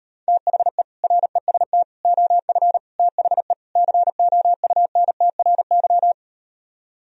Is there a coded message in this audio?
Yes: Morse code from 0.3 to 6.1 s, heard as a keyed tone, about 38 wpm, 700 Hz. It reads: THE REST OF THE COUNTRY